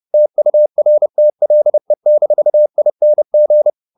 Morse code, heard as a keyed tone, about 30 wpm, 600 Hz. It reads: TURTLE-ING
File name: the old man and the sea / the old man and the sea 1 30wpm